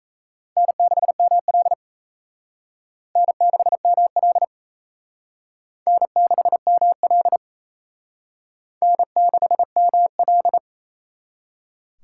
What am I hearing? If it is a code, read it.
Morse code, heard as a keyed tone, about 36 wpm, 700 Hz. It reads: N6ML N6ML N6ML N6ML